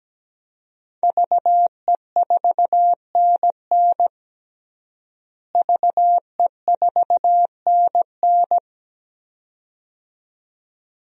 Morse code, heard as a keyed tone, about 17 wpm, 700 Hz. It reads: VE4NN VE4NN
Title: DXCC-100 Call Signs - 5 Characters - Repeated 2x - 17wpm